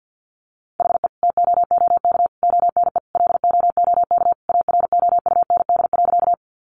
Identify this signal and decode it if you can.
Morse code, heard as a keyed tone, about 50 wpm, 700 Hz. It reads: HE TOOK ONE LOOK AROUND.
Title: Sentences from Top 600 Words 50wpm